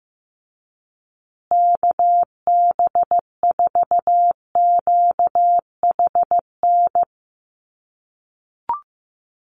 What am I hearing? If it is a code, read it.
Morse code, heard as a keyed tone, about 15 wpm, 700 Hz. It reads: KB4QHN
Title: USA Call Signs Repeated 3x 15wpm